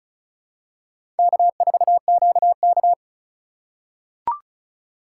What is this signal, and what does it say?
Morse code, heard as a keyed tone, about 36 wpm, 700 Hz. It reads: K4QK
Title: USA Call Signs - 3x Character Spacing - 35wpm